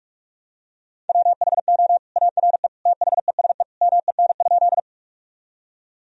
Morse code, heard as a keyed tone, about 46 wpm, 700 Hz. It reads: WHO ARE THESE MEN?
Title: Sentences from Top 1000 Words - Review Part 1 - 45wpm